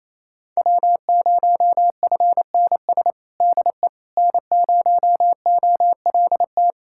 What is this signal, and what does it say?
Morse code, heard as a keyed tone, about 28 wpm, 700 Hz. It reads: W0FNS DE N0OLT
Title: USA Callsign DE Callsign - 5 Characters - 28wpm